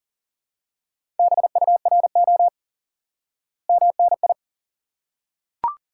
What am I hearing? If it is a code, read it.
Morse code, heard as a keyed tone, about 37 wpm, 700 Hz. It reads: BURO MNI
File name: IQE Course 3 - Lesson 89 - YAGI - Two Elements - 40wpm